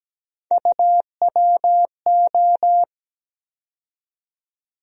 Morse code, heard as a keyed tone, about 17 wpm, 700 Hz. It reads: UWO
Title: USA Call Sign Postfixes - 3 Letters 17wpm